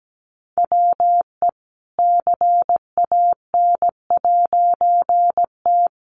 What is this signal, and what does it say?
Morse code, heard as a keyed tone, about 17 wpm, 700 Hz. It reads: WE CAN'T
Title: A Sound Of Thunder 17wpm